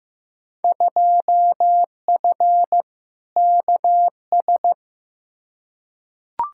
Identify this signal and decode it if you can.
Morse code, heard as a keyed tone, about 15 wpm, 700 Hz. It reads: 2F KS
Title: Field Day Exchanges Encore - 15wpm